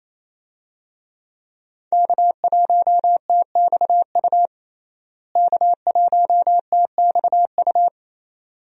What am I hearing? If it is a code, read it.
Morse code, heard as a keyed tone, about 28 wpm, 700 Hz. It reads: K1TXU K1TXU
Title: USA Call Signs - 5 Characters - Repeated 2x - 28wpm